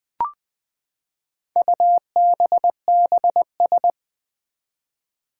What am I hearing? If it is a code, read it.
Morse code, heard as a keyed tone, about 20 wpm, 700 Hz. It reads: UBBS
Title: Four Challenging Letter-Number Combinations 20wpm